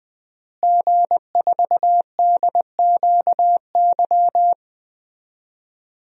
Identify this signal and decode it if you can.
Morse code, heard as a keyed tone, about 20 wpm, 700 Hz. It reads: G4DQY
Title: DXCC-100 Call Signs - 5 Characters - Repeated 2x - 20wpm